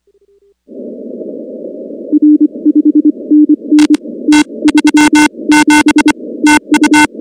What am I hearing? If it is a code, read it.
Morse code, heard as a keyed tone, about 25 wpm, spaced as for 22 wpm, 300 Hz. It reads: R 5NN T37 TU
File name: contests / 2019-cw-wpx-kt5j-k5pi- -2330 0-19